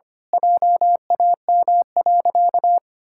Morse code, heard as a keyed tone, about 25 wpm, 700 Hz. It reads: JAM.
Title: Pangram Challenge v2 - 25wpm